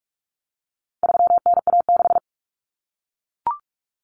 Morse code, heard as a keyed tone, about 44 wpm, 700 Hz. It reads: 3NA6
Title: Common Cut Numbers - 4-Digit 45wpm